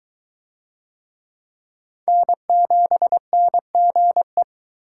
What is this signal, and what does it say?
Morse code, heard as a keyed tone, about 23 wpm, 700 Hz. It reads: N7NGE